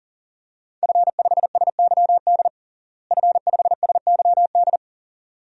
Morse code, heard as a keyed tone, about 40 wpm, 700 Hz. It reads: F5SYD F5SYD
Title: DXCC-100 Call Signs - 5 Characters - Repeated 2x - 40wpm